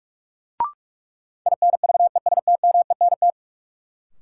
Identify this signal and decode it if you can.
Morse code, heard as a keyed tone, about 46 wpm, 700 Hz. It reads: INVESTMENT
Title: Sentences from Top 700 Words - New Word Drill - 45wpm